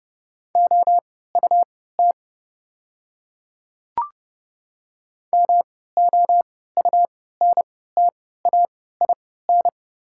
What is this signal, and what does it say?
Morse code, heard as a keyed tone, about 30 wpm, spaced as for 17 wpm, 690 Hz. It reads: OUT MOUNTAIN
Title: 3x Character Spacing - Top 500 Words - 30wpm